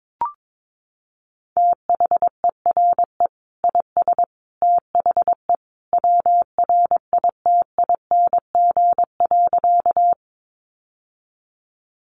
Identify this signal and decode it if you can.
Morse code, heard as a keyed tone, about 22 wpm, 700 Hz. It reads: THERE IS THE WRITING.
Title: Sentences from Top 800 Words 22wpm